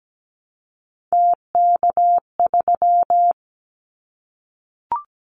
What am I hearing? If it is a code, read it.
Morse code, heard as a keyed tone, about 17 wpm, 700 Hz. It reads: TK3
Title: Callsign Prefixes for Top 100 DXCC - 17wpm